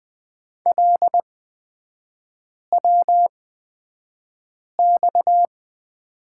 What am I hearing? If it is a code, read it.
Morse code, heard as a keyed tone, about 20 wpm, spaced as for 5 wpm, 700 Hz. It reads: LWX
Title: 229 Lesson 37e - Letter x - Three Characters 20wpm